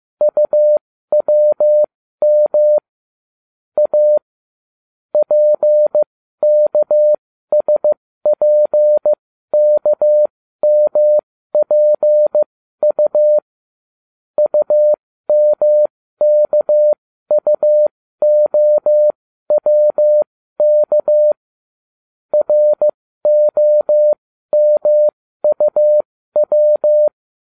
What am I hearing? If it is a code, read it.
Morse code, heard as a keyed tone, about 15 wpm, spaced as for 12 wpm, 600 Hz. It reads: UWM A PKSPKMPU UMKUOWK ROMUW